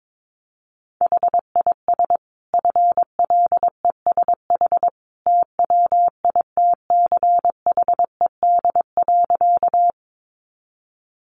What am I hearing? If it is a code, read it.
Morse code, heard as a keyed tone, about 22 wpm, 700 Hz. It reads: HIS FLESH TWITCHED.
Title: A Sound Of Thunder 22wpm